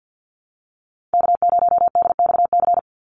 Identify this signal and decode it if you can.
Morse code, heard as a keyed tone, about 54 wpm, 700 Hz. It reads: K0DXC
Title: Call Signs Repeated 3x - 50wpm